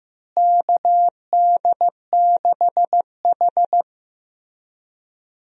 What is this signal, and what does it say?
Morse code, heard as a keyed tone, about 15 wpm, 700 Hz. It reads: KD6H
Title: USA Call Signs - 4 Characters - Repeated 3x - 15wpm